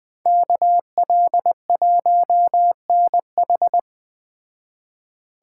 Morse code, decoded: KL1NH

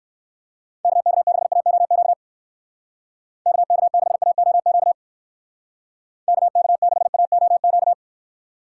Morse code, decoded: KK6AOQ KK6AOQ KK6AOQ

